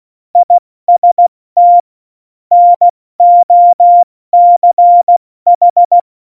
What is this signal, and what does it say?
Morse code, heard as a keyed tone, about 16 wpm, 700 Hz. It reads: IST NOCH
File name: Morsekurs / Witze / Witz 06. Oktober 2020, 70 F80